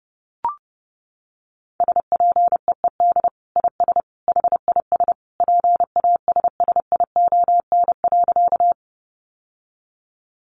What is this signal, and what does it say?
Morse code, heard as a keyed tone, about 30 wpm, 700 Hz. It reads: SPEED IS HIS PASSION.
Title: Sentences from Top 1300 Words 30wpm